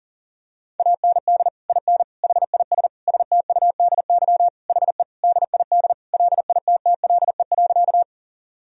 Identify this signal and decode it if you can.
Morse code, heard as a keyed tone, about 40 wpm, 700 Hz. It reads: AND IN HIS STUDY HE DID LITTLE.
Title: Sentences from Top 1000 Words - Review Part 1 - 40wpm